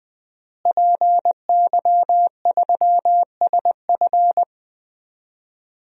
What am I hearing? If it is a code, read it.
Morse code, heard as a keyed tone, about 20 wpm, 700 Hz. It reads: PY3SF
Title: DXCC-100 Call Signs - 5 Characters - Repeated 2x - 20wpm